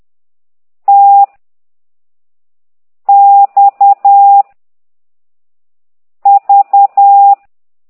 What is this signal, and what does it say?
Morse code, heard as a keyed tone, about 10 wpm, 800 Hz. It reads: T X V
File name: exerc7